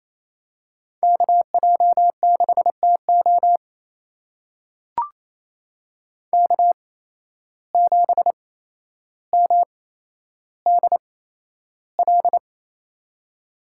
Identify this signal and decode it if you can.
Morse code, heard as a keyed tone, about 28 wpm, spaced as for 12 wpm, 700 Hz. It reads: KJ6TO K7MDL
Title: USA Call Signs - 5 Characters - 8x Character Spacing - 28wpm